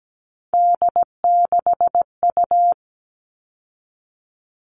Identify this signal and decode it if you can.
Morse code, heard as a keyed tone, about 17 wpm, 700 Hz. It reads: D6U
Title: All Cut Numbers - 3-Digit 17wpm